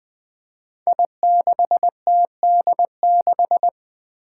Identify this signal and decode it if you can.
Morse code, heard as a keyed tone, about 20 wpm, 700 Hz. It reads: I6TD6
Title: Five Challenging Letter-Number Combinations 20wpm